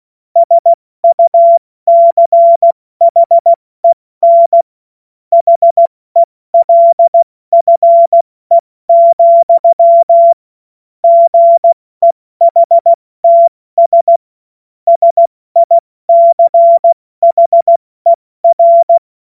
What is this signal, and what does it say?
Morse code, heard as a keyed tone, about 16 wpm, 680 Hz. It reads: SUCHEN HELFE, GEHTS SICHER
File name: Morsekurs / Witze / Witz 20. Januar 2026 70 F80